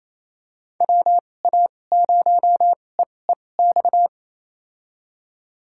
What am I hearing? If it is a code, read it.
Morse code, heard as a keyed tone, about 28 wpm, spaced as for 20 wpm, 700 Hz. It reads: WA0EEX
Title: USA Call Signs - 6 Characters - 2x Character Spacing - 28wpm